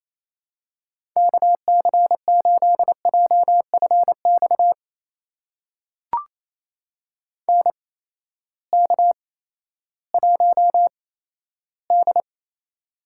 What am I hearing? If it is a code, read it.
Morse code, heard as a keyed tone, about 28 wpm, spaced as for 14 wpm, 700 Hz. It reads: KC8JFX NK1D
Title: USA Call Signs - 8x Character Spacing - 28wpm